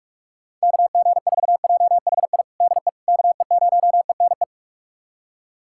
Morse code, heard as a keyed tone, about 46 wpm, 700 Hz. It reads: KG4JHI DE KE0ENE
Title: USA Callsign DE Callsign 45wpm